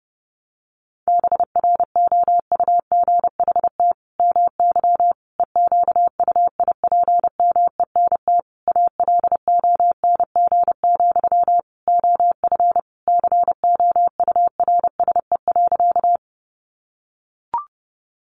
Morse code, heard as a keyed tone, about 30 wpm, 700 Hz. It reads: BROUGHT MY EQUIPMENT ALONG, OF COURSE.
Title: Sentences from Top 700 Words 30wpm